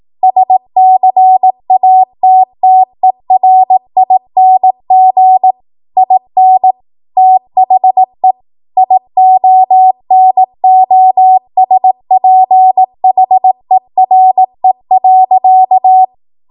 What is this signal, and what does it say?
Morse code, heard as a keyed tone, about 18 wpm, 750 Hz. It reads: SCATTERING IN THE IONOSPHERE.
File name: Morse / 18WPM